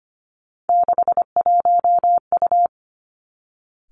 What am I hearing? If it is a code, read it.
Morse code, heard as a keyed tone, about 25 wpm, 700 Hz. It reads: 61U
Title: Three Challenging Letter-Number Combinations 25wpm